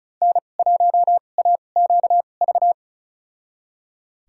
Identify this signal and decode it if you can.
Morse code, heard as a keyed tone, about 36 wpm, spaced as for 25 wpm, 700 Hz. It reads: N1AQV
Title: USA Call Signs - 2x Character Spacing - 35wpm